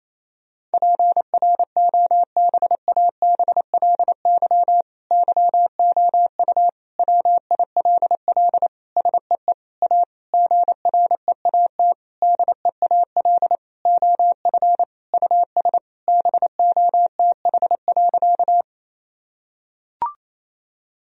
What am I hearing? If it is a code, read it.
Morse code, heard as a keyed tone, about 28 wpm, 700 Hz. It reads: PROBABLY YOU WILL SEE A GREAT DEAL OF US BOTH.
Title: Sentences from Top 1000 Words - Review Part 4 - 28wpm